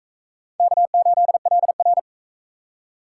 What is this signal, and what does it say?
Morse code, heard as a keyed tone, about 37 wpm, 700 Hz. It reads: K8LR